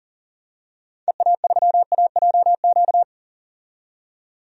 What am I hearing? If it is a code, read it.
Morse code, heard as a keyed tone, about 40 wpm, 700 Hz. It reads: EA3AJQ
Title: DXCC-100 Call Signs - 6 Characters - 40wpm